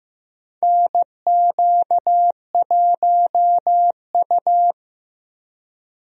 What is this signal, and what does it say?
Morse code, heard as a keyed tone, about 15 wpm, 700 Hz. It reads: NQ1U
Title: USA Call Signs - 4 Characters - Repeated 2x - 15wpm